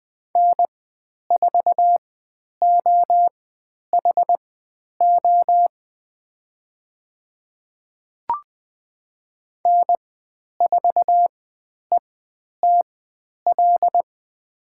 Morse code, heard as a keyed tone, about 20 wpm, spaced as for 10 wpm, 700 Hz. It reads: N4OHO N4ETL
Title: 109 Lesson 18g - Letter m - Call Signs 20wpm